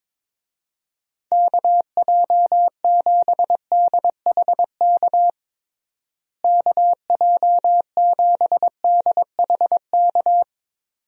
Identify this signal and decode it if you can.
Morse code, heard as a keyed tone, about 22 wpm, 700 Hz. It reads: KJ7DHK KJ7DHK
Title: USA Call Signs - 6 Characters - Repeated 2x - 22wpm